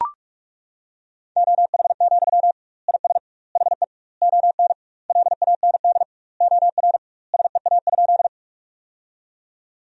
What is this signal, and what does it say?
Morse code, heard as a keyed tone, about 42 wpm, 700 Hz. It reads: OH, IS HE ON LAND OR SEA?
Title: 2x Word-Spacing - Sentences from Top 500 Words - 45wpm